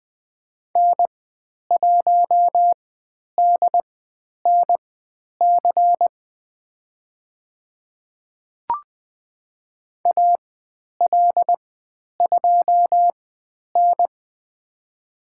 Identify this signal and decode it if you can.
Morse code, heard as a keyed tone, about 20 wpm, spaced as for 10 wpm, 700 Hz. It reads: N1DNC AL2N
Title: 095 Lesson 16g - Letter c - Call Signs 20wpm